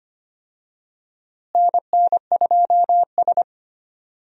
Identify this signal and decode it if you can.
Morse code, heard as a keyed tone, about 25 wpm, 700 Hz. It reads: NN2S
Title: USA Call Signs - 4 Characters - Repeated 3x - 25wpm